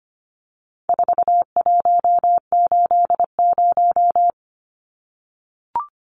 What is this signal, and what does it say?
Morse code, heard as a keyed tone, about 25 wpm, 700 Hz. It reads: 4180
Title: Numbers - 4-Digit 25wpm